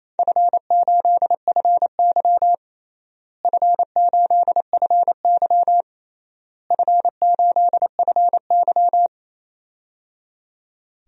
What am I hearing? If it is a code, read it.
Morse code, heard as a keyed tone, about 28 wpm, 700 Hz. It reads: F8FY F8FY F8FY
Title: DXCC-100 Call Signs - 4 Characters - Repeated 3x - 28wpm